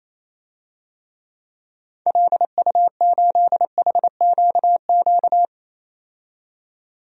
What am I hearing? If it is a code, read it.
Morse code, heard as a keyed tone, about 28 wpm, 700 Hz. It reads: LU8HQQ
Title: DXCC-100 Call Signs - 6 Characters - 28wpm